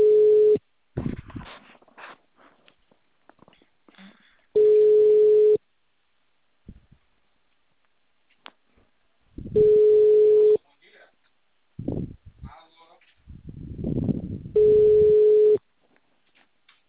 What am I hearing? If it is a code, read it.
Ringback tone (Europe, 425 Hz), with speech.
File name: out- -2046- - - .1882